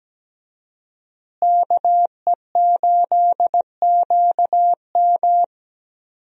Morse code, heard as a keyed tone, about 17 wpm, 700 Hz. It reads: KE8QM
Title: USA Call Signs - 5 Characters - Repeated 3x - 17wpm